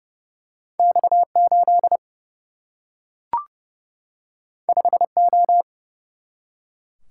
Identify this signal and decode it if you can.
Morse code, heard as a keyed tone, about 30 wpm, 700 Hz. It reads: X 8 5 O
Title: Single-Letter Single-Number 30wpm